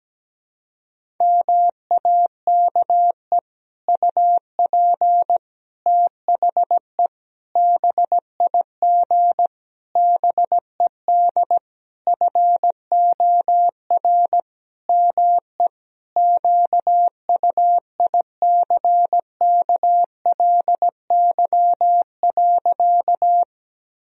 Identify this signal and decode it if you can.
Morse code, heard as a keyed tone, about 17 wpm, 700 Hz. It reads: MAKE UP THE BIG BED FOR ME QUICKLY.
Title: Sentences from Top 1000 Words - Review Part 1 - 17wpm